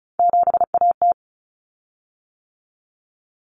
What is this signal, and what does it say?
Morse code, heard as a keyed tone, about 34 wpm, 700 Hz. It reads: 7AT
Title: Common Cut Numbers - 3-Digit 35wpm